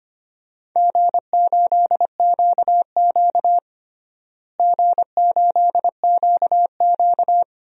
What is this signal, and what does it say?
Morse code, heard as a keyed tone, about 25 wpm, 700 Hz. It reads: G8QQ G8QQ
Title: DXCC-100 Call Signs - 4 Characters - Repeated 2x - 25wpm